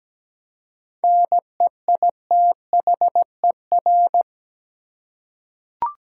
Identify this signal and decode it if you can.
Morse code, heard as a keyed tone, about 17 wpm, 700 Hz. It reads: NEITHER
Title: Seven Letter Words - 17wpm